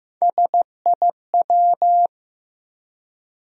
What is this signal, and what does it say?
Morse code, heard as a keyed tone, about 15 wpm, 700 Hz. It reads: SIW
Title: USA Call Sign Postfixes - 3 Letters 15wpm